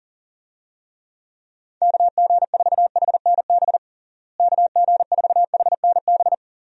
Morse code, heard as a keyed tone, about 40 wpm, 700 Hz. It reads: KG4HNB KG4HNB
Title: USA Call Signs - 6 Characters - Repeated 3x - 40wpm